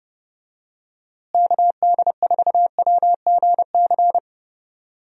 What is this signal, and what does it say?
Morse code, heard as a keyed tone, about 32 wpm, 700 Hz. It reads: KD4WGC
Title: USA Call Signs - 6 Characters - Repeated 3x - 30wpm